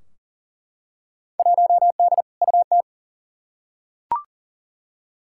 Morse code, heard as a keyed tone, about 40 wpm, 700 Hz. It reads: 1D UT
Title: Field Day Exchanges - 40wpm